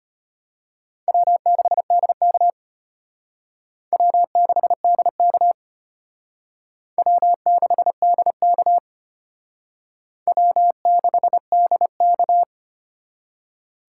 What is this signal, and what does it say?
Morse code, heard as a keyed tone, about 31 wpm, 700 Hz. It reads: W6DK W6DK W6DK W6DK
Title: Speed Racing - USA 1x2 Call Signs 25wpm